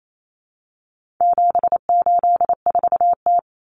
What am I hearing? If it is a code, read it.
Morse code, heard as a keyed tone, about 28 wpm, 700 Hz. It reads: 784T
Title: Common Cut Numbers - 4-Digit 28wpm